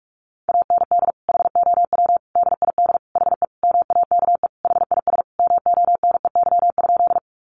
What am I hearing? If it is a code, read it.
Morse code, heard as a keyed tone, about 44 wpm, 700 Hz. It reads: AND HOW DID HE MAKE HIS MONEY?
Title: Sentences from Top 400 Words 45wpm